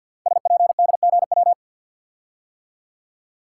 Morse code, heard as a keyed tone, about 46 wpm, 700 Hz. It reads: SPDGW
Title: Code Groups - 5 Letters - 50wpm